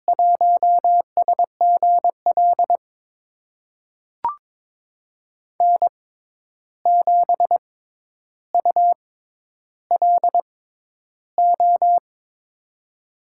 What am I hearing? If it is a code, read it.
Morse code, heard as a keyed tone, about 21 wpm, spaced as for 10 wpm, 700 Hz. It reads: 1SGL N7ULO
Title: USA Call Signs - 5 Characters - 6x Character Spacing - 22wpm